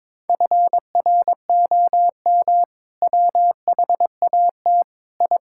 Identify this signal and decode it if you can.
Morse code, heard as a keyed tone, about 22 wpm, 700 Hz. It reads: FROM WHAT I
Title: Sentences from Top 1000 Words - Review Part 6 - 22wpm